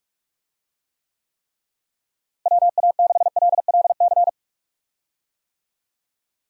Morse code, heard as a keyed tone, about 43 wpm, 700 Hz. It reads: WA6LLC
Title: USA Call Signs - 6 Characters - Repeated 3x - 45wpm